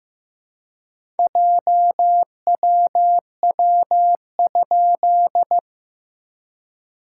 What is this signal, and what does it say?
Morse code, heard as a keyed tone, about 15 wpm, 700 Hz. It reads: JWW?
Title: SOTA QSO Practice 15wpm